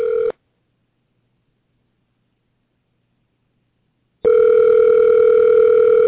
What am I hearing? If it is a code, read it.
Ringback tone (North America), 440 + 480 Hz.